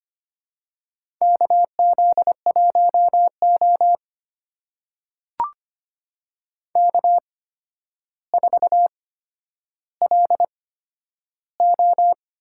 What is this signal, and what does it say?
Morse code, heard as a keyed tone, about 25 wpm, spaced as for 11 wpm, 700 Hz. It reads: KZ1O K4LO
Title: USA Call Signs - 4 Characters - 8x Character Spacing - 25wpm